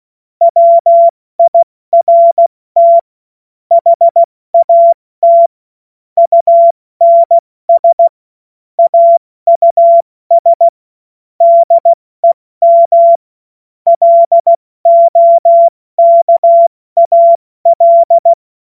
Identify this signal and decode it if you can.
Morse code, heard as a keyed tone, about 16 wpm, spaced as for 14 wpm, 680 Hz. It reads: WIRT HAT UNS AUS DEM LOKAL
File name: Morsekurs / Witze / Witz 19. Oktober 2021, 70 F80